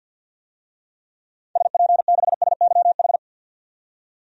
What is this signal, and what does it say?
Morse code, heard as a keyed tone, about 50 wpm, 700 Hz. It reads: SP6SYH
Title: DXCC-100 Call Signs - 6 Characters - 50wpm